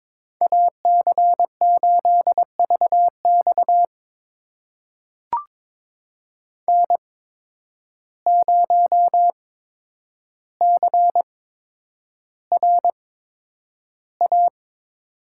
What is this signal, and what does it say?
Morse code, heard as a keyed tone, about 22 wpm, spaced as for 9 wpm, 700 Hz. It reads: AC8VX N0CRA
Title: USA Call Signs - 5 Characters - 8x Character Spacing - 22wpm